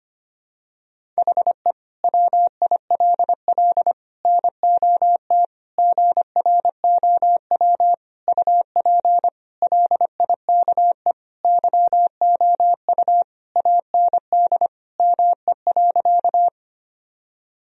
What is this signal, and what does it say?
Morse code, heard as a keyed tone, about 25 wpm, 700 Hz. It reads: HE WILL NOT GROW UP LIKE YOU AND ME.
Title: Sentences from Top 1000 Words - Review Part 1 - 25wpm